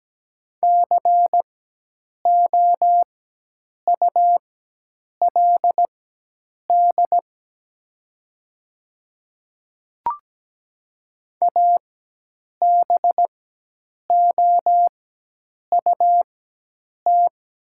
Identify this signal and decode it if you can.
Morse code, heard as a keyed tone, about 17 wpm, spaced as for 8 wpm, 700 Hz. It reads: COULD ABOUT
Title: 4x Character Spacing - Top 100 Words - 17wpm